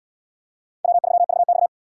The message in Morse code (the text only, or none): W3VX